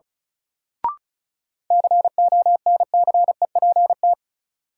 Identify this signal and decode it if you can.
Morse code, heard as a keyed tone, about 34 wpm, 700 Hz. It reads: CONCEPT
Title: Sentences from Top 1100 Words - New Word Drill - 35wpm